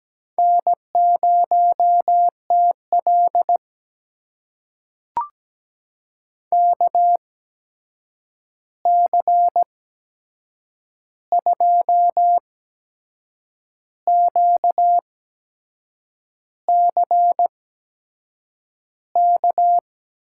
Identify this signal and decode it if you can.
Morse code, heard as a keyed tone, about 17 wpm, spaced as for 6 wpm, 700 Hz. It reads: N0TL KC2QCK
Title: USA Call Signs - 8x Character Spacing - 17wpm